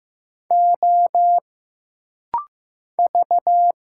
Morse code, heard as a keyed tone, about 15 wpm, 700 Hz. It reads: O V
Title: Single Letters - Rapid-Fire 15wpm